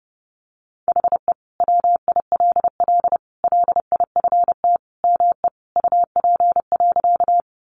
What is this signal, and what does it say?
Morse code, heard as a keyed tone, about 30 wpm, 700 Hz. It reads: HE WILL LIFT ME UP.
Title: Sentences from Top 1000 Words 30wpm